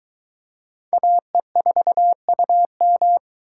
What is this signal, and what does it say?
Morse code, heard as a keyed tone, about 22 wpm, 700 Hz. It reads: AE4UM